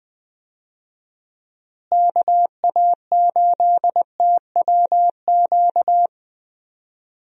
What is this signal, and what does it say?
Morse code, heard as a keyed tone, about 20 wpm, 700 Hz. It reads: KA8TWQ
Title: USA Call Signs - 6 Characters - Repeated 3x - 20wpm